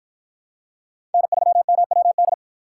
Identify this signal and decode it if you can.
Morse code, heard as a keyed tone, about 51 wpm, 700 Hz. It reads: N3MWD